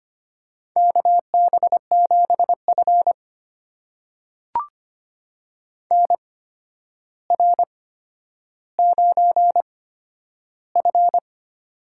Morse code, heard as a keyed tone, about 25 wpm, spaced as for 11 wpm, 700 Hz. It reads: KB7F NR9F
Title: USA Call Signs - 4 Characters - 8x Character Spacing - 25wpm